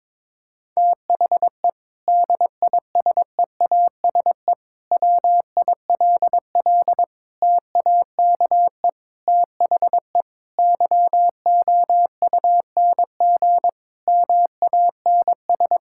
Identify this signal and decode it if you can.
Morse code, heard as a keyed tone, about 22 wpm, 700 Hz. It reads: THE DISEASE WILL TAKE THE YOUNG MANS
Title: Sentences from Top 1000 Words - Review Part 6 - 22wpm